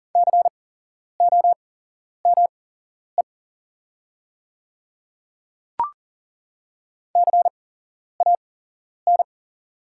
Morse code, heard as a keyed tone, about 40 wpm, spaced as for 11 wpm, 700 Hz. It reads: COME CAN
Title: 8x Character Spacing - Top 100 Words - 40wpm